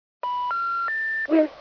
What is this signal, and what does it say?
Special information tone, with noise.